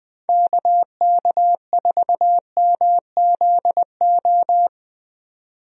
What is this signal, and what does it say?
Morse code, heard as a keyed tone, about 20 wpm, 700 Hz. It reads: KK4MZO
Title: USA Call Signs - 6 Characters - 20wpm